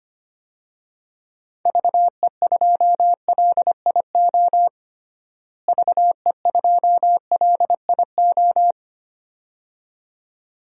Morse code, heard as a keyed tone, about 25 wpm, 700 Hz. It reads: VE2LIO VE2LIO
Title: DXCC-100 Call Signs - 6 Characters - Repeated 2x - 25wpm